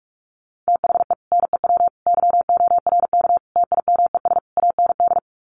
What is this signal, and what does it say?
Morse code, heard as a keyed tone, about 44 wpm, 700 Hz. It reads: THE NEW YORK TIMES AND
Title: Sets of 5 Words Encore 45wpm